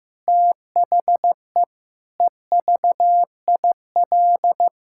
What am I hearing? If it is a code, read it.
Morse code, heard as a keyed tone, about 15 wpm, 700 Hz. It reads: THE EVIL
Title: Pangram Challenge v2 - 15wpm